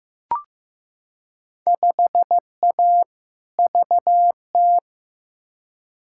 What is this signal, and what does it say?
Morse code, heard as a keyed tone, about 15 wpm, 700 Hz. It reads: 5A VT
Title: Field Day Exchanges Encore - 15wpm